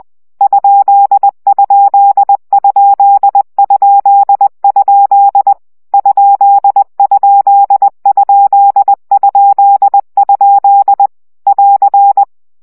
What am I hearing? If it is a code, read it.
Morse code, heard as a keyed tone, about 20 wpm, 800 Hz. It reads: ????? ????? <AR>
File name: CW gyakorlas kerdojel 17 baud